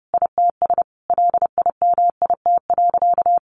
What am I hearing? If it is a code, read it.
Morse code, heard as a keyed tone, about 30 wpm, 700 Hz. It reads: ITS LIMIT.
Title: Sentences from Top 1500 Words 30wpm